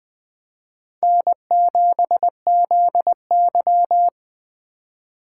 Morse code, heard as a keyed tone, about 20 wpm, 700 Hz. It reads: N7ZY
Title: USA Call Signs - 4 Characters - 2x Character Spacing - 20wpm